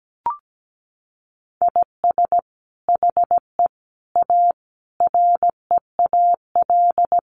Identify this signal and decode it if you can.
Morse code, heard as a keyed tone, about 17 wpm, 700 Hz. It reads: IS HE A REAL
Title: Sentences from Top 300 Words 17wpm